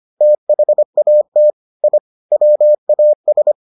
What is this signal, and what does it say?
Morse code, heard as a keyed tone, about 25 wpm, 600 Hz. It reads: THAT I WAS